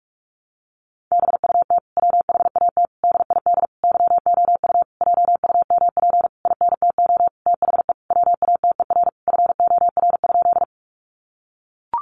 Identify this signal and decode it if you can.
Morse code, heard as a keyed tone, about 44 wpm, 700 Hz. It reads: BUT WHAT DID YOU JUMP INTO THE WATER FOR?
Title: Sentences from Top 1300 Words 45wpm